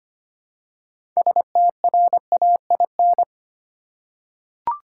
Morse code, heard as a keyed tone, about 25 wpm, 700 Hz. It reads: STRAIN
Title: Six Letter Words - 25wpm